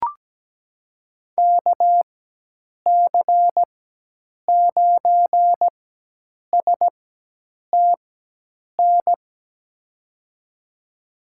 Morse code, heard as a keyed tone, about 17 wpm, spaced as for 8 wpm, 700 Hz. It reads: KC9STN
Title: USA Call Signs - 6 Characters - 4x Character Spacing - 17wpm